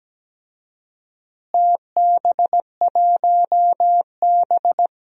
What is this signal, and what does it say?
Morse code, heard as a keyed tone, about 17 wpm, 700 Hz. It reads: TB1B
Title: DXCC-100 Call Signs - 4 Characters - Repeated 2x - 17wpm